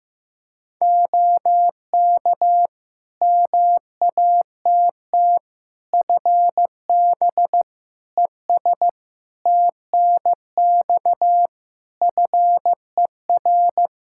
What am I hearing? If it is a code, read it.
Morse code, heard as a keyed tone, about 15 wpm, 700 Hz. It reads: OK MATT FB ES TNX FER
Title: Practice QSO - Exchange 2 - Info - 15wpm